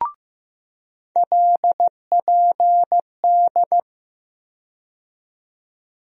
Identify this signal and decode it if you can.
Morse code, heard as a keyed tone, about 15 wpm, 700 Hz. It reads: LPD
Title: USA Call Sign Postfixes - 3 Letters 15wpm